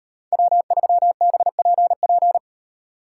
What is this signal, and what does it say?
Morse code, heard as a keyed tone, about 32 wpm, 700 Hz. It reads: W3BPP